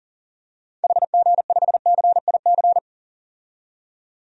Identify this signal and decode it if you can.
Morse code, heard as a keyed tone, about 37 wpm, 700 Hz. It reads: HG5CIC
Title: DXCC-100 Call Signs - 6 Characters - 40wpm